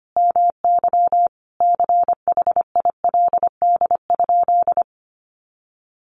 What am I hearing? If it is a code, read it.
Morse code, heard as a keyed tone, about 25 wpm, 700 Hz. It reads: MY CHILD?
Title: Sentences from Top 800 Words 25wpm